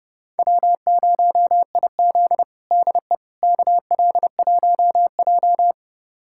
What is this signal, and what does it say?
Morse code, heard as a keyed tone, about 30 wpm, 700 Hz. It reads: W0IZ DE KL1J
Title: USA Callsign DE Callsign - 4 Characters - 30wpm